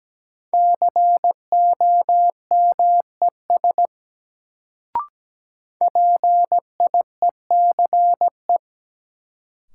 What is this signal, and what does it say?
Morse code, heard as a keyed tone, about 17 wpm, 700 Hz. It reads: COMES PIECE
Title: Sets of 3 Words - New Word Drill - 17wpm